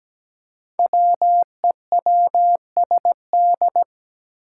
Morse code, heard as a keyed tone, about 17 wpm, 700 Hz. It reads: WEWSD
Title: Five Challenging Letter-Number Combinations 17wpm